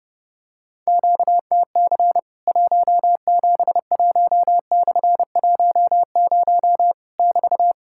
Morse code, heard as a keyed tone, about 30 wpm, 700 Hz. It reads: QTC 171/10 <BT>
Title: Worked All Europe DX Contest - QTC Exchange - 30wpm